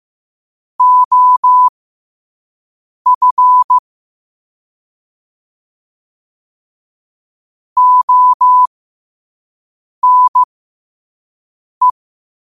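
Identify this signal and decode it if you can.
Morse code, heard as a keyed tone, about 15 wpm, spaced as for 5 wpm, 1000 Hz. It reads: OF ONE
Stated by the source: Morse code at 5 words per minute.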